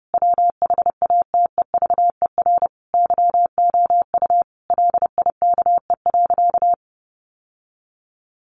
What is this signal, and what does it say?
Morse code, heard as a keyed tone, about 30 wpm, 700 Hz. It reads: WHATEVER YOU LIKE.
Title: Sentences from Top 1100 Words 30wpm